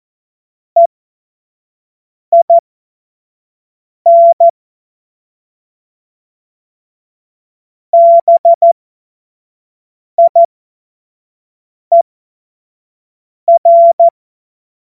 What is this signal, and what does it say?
Morse code, heard as a keyed tone, about 14 wpm, spaced as for 5 wpm, 680 Hz. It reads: EIN BIER
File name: Morsekurs / Witze / Witz 22. Juni 2021, 25 F70